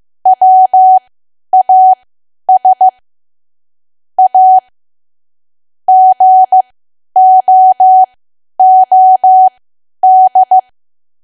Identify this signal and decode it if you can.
Morse code, heard as a keyed tone, about 15 wpm, spaced as for 10 wpm, 750 Hz. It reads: WAS A GOOD
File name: Archive / 10 WPM / 10WPM